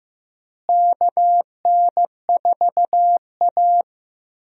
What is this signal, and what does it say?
Morse code, heard as a keyed tone, about 15 wpm, 700 Hz. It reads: KN4A
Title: USA Call Signs - 4 Characters - Repeated 3x - 15wpm